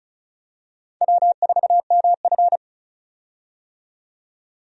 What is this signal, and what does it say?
Morse code, heard as a keyed tone, about 34 wpm, 700 Hz. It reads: W4MF
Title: USA Call Signs - 4 Characters - 35wpm